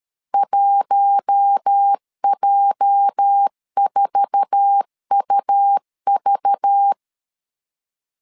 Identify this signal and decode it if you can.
Morse code, heard as a keyed tone, about 13 wpm, 800 Hz. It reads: 1J4UV
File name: LEZIONI / 10 SIGNS / 10D COMA 12wpm 2544